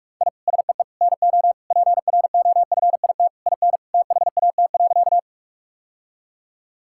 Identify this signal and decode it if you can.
Morse code, heard as a keyed tone, about 44 wpm, 700 Hz. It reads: I SEE NO PROFIT IN THAT.
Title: Sentences from Top 1000 Words - Review Part 5 - 45wpm